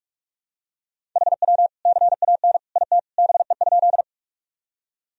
Morse code, heard as a keyed tone, about 42 wpm, 700 Hz. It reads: HW CAN IT BE?
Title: Sentences from Top 100 Words and Abbreviations 45wpm